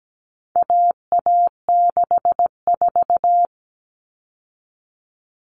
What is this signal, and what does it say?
Morse code, heard as a keyed tone, about 17 wpm, 700 Hz. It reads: AA64
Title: All Cut Numbers - 4-Digit 17wpm